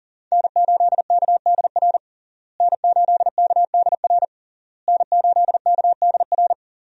Morse code, heard as a keyed tone, about 40 wpm, 700 Hz. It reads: N8KDR N8KDR N8KDR
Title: USA Call Signs - 5 Characters - Repeated 3x - 40wpm